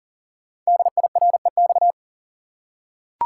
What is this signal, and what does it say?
Morse code, heard as a keyed tone, about 38 wpm, 690 Hz. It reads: DIREX
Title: Code Groups - 5 Letters - 40wpm